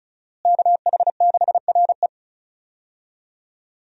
Morse code, heard as a keyed tone, about 36 wpm, 700 Hz. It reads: KH6RE
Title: USA Call Signs - 5 Characters - Repeated 2x - 35wpm